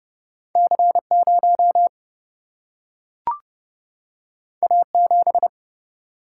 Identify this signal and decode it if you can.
Morse code, heard as a keyed tone, about 30 wpm, 700 Hz. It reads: C0 A7
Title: Single-Letter Single-Number 30wpm